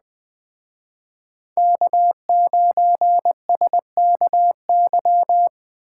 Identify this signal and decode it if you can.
Morse code, heard as a keyed tone, about 20 wpm, 700 Hz. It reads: K9SKY
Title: Speed Racing - USA 1x3 Call Signs 17wpm